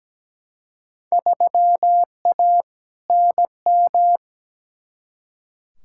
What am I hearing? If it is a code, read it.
Morse code, heard as a keyed tone, about 17 wpm, 700 Hz. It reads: 3A NM
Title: Field Day Exchanges Encore - 17wpm